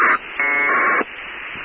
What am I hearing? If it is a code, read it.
Packet radio (AX.25, as in APRS).
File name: pcsat2 voice mode 31 aug 2006 part3